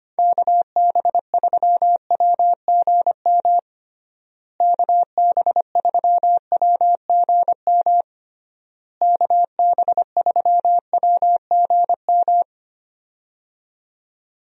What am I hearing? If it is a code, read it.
Morse code, heard as a keyed tone, about 25 wpm, 700 Hz. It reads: KB3WGM KB3WGM KB3WGM
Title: USA Call Signs - 6 Characters - Repeated 3x - 25wpm